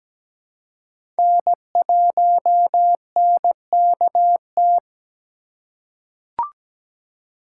Morse code, heard as a keyed tone, about 17 wpm, 700 Hz. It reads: N1NKT
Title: USA Call Signs - 3x Character Spacing - 17wpm